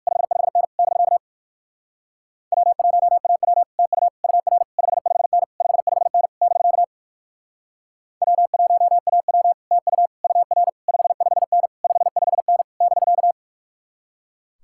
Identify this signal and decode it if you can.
Morse code, heard as a keyed tone, about 57 wpm, 700 Hz. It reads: 55N <BK> W1AW TU UR 55N 55N <BK> W1AW TU UR 55N 55N <BK>